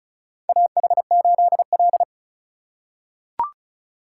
Morse code, heard as a keyed tone, about 34 wpm, 700 Hz. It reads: AH8L
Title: USA Call Signs - 4 Characters - 3x Character Spacing - 35wpm